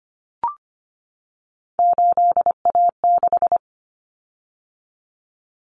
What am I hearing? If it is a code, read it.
Morse code, heard as a keyed tone, about 25 wpm, 700 Hz. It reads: 8A6
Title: Common Cut Numbers - 3-Digit 25wpm